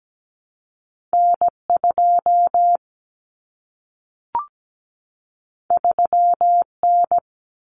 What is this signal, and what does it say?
Morse code, heard as a keyed tone, about 17 wpm, 700 Hz. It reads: N2 3N
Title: Common Cut Numbers - 2-Digit 17wpm